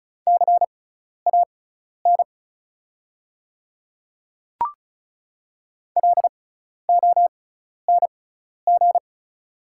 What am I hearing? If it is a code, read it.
Morse code, heard as a keyed tone, about 34 wpm, spaced as for 12 wpm, 700 Hz. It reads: CAN LONG
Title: 6x Character Spacing - Top 200 Words - 35wpm